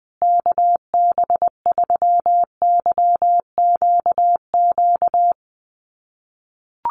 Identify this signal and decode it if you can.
Morse code, heard as a keyed tone, about 20 wpm, 700 Hz. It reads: KB3YQQ
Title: USA Call Signs Repeated 3x 20wpm